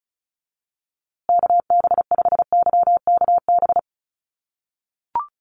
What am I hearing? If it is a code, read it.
Morse code, heard as a keyed tone, about 34 wpm, 700 Hz. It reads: KB5YKB
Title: USA Call Signs Repeated 2x 35wpm